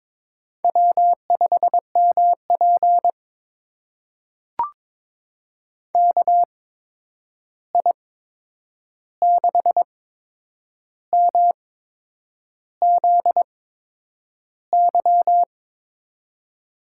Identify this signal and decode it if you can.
Morse code, heard as a keyed tone, about 22 wpm, spaced as for 8 wpm, 700 Hz. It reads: W5MP KI6MZY
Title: USA Call Signs - 8x Character Spacing - 22wpm